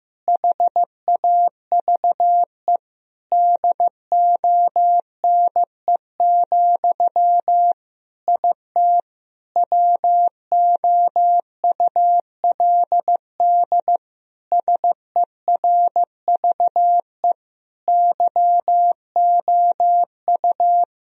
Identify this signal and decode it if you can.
Morse code, heard as a keyed tone, about 15 wpm, 700 Hz. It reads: HAVE DONE, IT WOULD SERVE YOU
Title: Sentences from Top 1000 Words - Review Part 2 - 15wpm